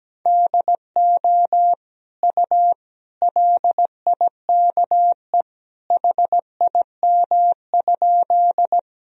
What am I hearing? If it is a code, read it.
Morse code, heard as a keyed tone, about 17 wpm, 700 Hz. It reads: DO U LIKE HIM?
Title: Sentences from Top 100 Words and Abbreviations 17wpm